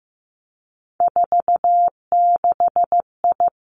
Morse code, heard as a keyed tone, about 15 wpm, 700 Hz. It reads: 46I
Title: Three Challenging Letter-Number Combinations 15wpm